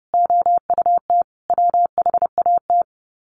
Morse code, heard as a keyed tone, about 30 wpm, 700 Hz. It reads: OUT WHAT
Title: Sets of 5 Words 30wpm